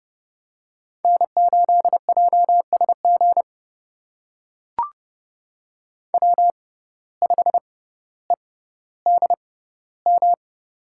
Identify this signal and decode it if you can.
Morse code, heard as a keyed tone, about 30 wpm, spaced as for 15 wpm, 700 Hz. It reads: N8JSG W5EDM
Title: USA Call Signs - 5 Characters - 6x Character Spacing - 30wpm